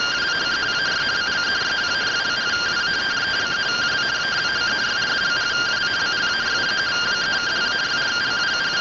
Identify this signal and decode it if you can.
Two-tone teleprinter signal, about 170 Hz shift.
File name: findings / CT SAGITAL 27.gen-22 rtty rx